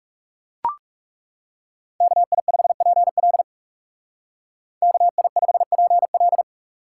Morse code, heard as a keyed tone, about 36 wpm, 700 Hz. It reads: KI5PL KI5PL
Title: Speed Racing - USA 2x2 Call Signs 30wpm